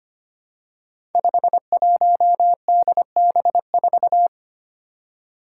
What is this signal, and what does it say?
Morse code, heard as a keyed tone, about 25 wpm, 700 Hz. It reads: 51DB4
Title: Five Challenging Letter-Number Combinations 25wpm